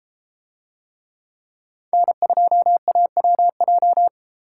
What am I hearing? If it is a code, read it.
Morse code, heard as a keyed tone, about 32 wpm, 700 Hz. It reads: N2AWJ